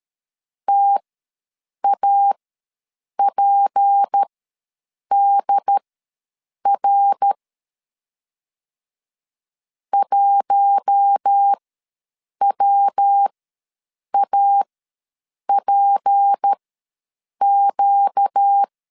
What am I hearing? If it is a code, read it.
Morse code, heard as a keyed tone, about 13 wpm, spaced as for 7 wpm, 800 Hz. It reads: TAPDR 1WAPQ
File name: LEZIONI / 6 PKX0 / 6 P ws 12wpm 3117